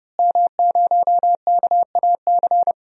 Morse code, heard as a keyed tone, about 30 wpm, 700 Hz. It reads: M0KAC
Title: DXCC-100 Call Signs - 5 Characters - Repeated 3x - 30wpm